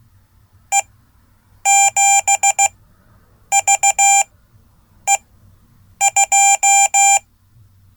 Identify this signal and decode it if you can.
Morse code, heard as a keyed tone, about 16 wpm, spaced as for 8 wpm, 790 Hz. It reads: E7VE2
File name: Code / 5min-a